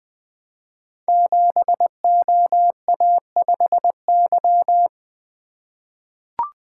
Morse code, heard as a keyed tone, about 20 wpm, 700 Hz. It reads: 7OA5Y
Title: Code Groups - 5 Letter:Numbers - 20wpm